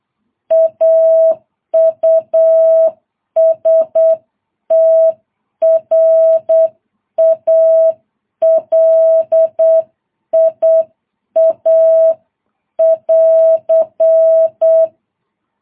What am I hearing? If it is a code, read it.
Morse code, heard as a keyed tone, about 8 wpm, 650 Hz. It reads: AUSTRALIA <AR>
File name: jh8cbh